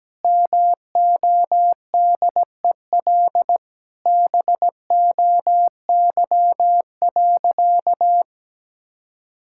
Morse code, heard as a keyed tone, about 17 wpm, 700 Hz. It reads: MODEL BOY.
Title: Sentences from Top 1000 Words - Review Part 5 - 17wpm